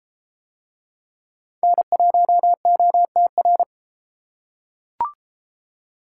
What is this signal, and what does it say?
Morse code, heard as a keyed tone, about 32 wpm, 700 Hz. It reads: N1OTR